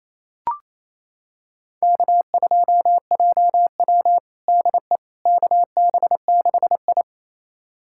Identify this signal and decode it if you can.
Morse code, heard as a keyed tone, about 28 wpm, 700 Hz. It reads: K2JW DE KB6I
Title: USA Callsign DE Callsign - 4 Characters - 28wpm